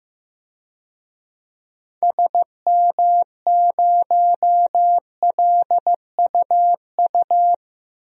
Morse code, heard as a keyed tone, about 15 wpm, 700 Hz. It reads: SM0LUU
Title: DXCC-100 Call Signs - 6 Characters - 15wpm